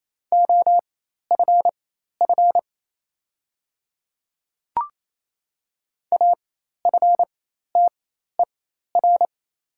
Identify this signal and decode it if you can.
Morse code, heard as a keyed tone, about 27 wpm, spaced as for 13 wpm, 700 Hz. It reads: OFF AFTER
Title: 4x Character Spacing - Top 200 Words - 28wpm